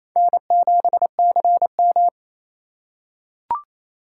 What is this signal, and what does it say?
Morse code, heard as a keyed tone, about 27 wpm, 700 Hz. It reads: N7CM
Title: USA Call Signs - 4 Characters - 4x Character Spacing - 28wpm